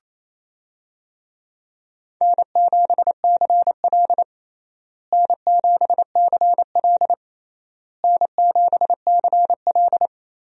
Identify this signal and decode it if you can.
Morse code, heard as a keyed tone, about 28 wpm, 700 Hz. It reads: N7CL N7CL N7CL
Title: USA Call Signs - 4 Characters - Repeated 3x - 28wpm